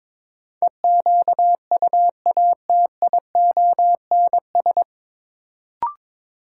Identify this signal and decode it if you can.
Morse code, heard as a keyed tone, about 22 wpm, 700 Hz. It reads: EQUATIONS
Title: Sets of 5 Words - New Word Drill - 22wpm